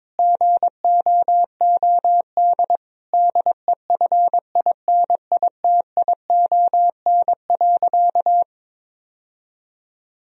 Morse code, heard as a keyed tone, about 22 wpm, 700 Hz. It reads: GOOD DEFINITION.
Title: Sentences from Top 1000 Words - Review Part 4 - 22wpm